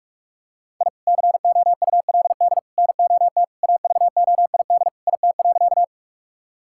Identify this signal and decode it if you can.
Morse code, heard as a keyed tone, about 45 wpm, 700 Hz. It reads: I COULD NOT AVOID IT.
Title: Sentences from Top 1000 Words - Review Part 6 - 45wpm